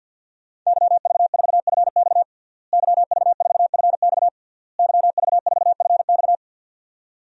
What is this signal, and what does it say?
Morse code, heard as a keyed tone, about 48 wpm, 700 Hz. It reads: YV4FX YV4FX YV4FX
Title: DXCC-100 Call Signs - 5 Characters - Repeated 3x - 50wpm